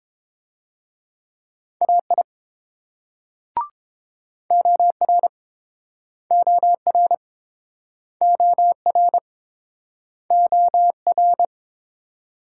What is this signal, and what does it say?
Morse code, heard as a keyed tone, about 28 wpm, 700 Hz. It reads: AI OR OR OR OR